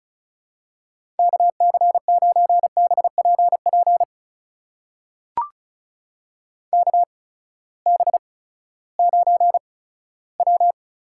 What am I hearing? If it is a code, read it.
Morse code, heard as a keyed tone, about 36 wpm, spaced as for 18 wpm, 700 Hz. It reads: KC9BPP KB9W
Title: USA Call Signs - 6 Characters - 8x Character Spacing - 35wpm